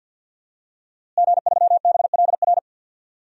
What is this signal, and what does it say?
Morse code, heard as a keyed tone, about 52 wpm, 700 Hz. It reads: G3BLR
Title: DXCC-100 Call Signs - 5 Characters - Repeated 3x - 50wpm